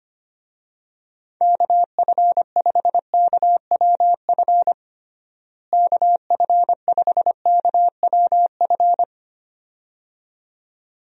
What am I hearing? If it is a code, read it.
Morse code, heard as a keyed tone, about 25 wpm, 700 Hz. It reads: KF5KWF KF5KWF
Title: USA Call Signs - 6 Characters - Repeated 2x - 25wpm